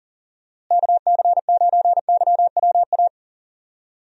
Morse code, heard as a keyed tone, about 40 wpm, 700 Hz. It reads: KC9YWA
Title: USA Call Signs - 6 Characters - Repeated 2x - 40wpm